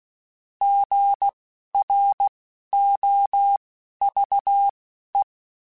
Morse code, heard as a keyed tone, about 16 wpm, spaced as for 11 wpm, 800 Hz. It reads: GROVE